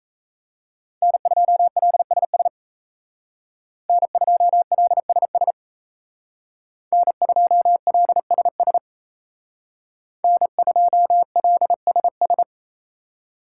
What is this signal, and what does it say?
Morse code, heard as a keyed tone, about 33 wpm, 700 Hz. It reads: N2LSS N2LSS N2LSS N2LSS
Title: Speed Racing - USA 1x3 Call Signs 28wpm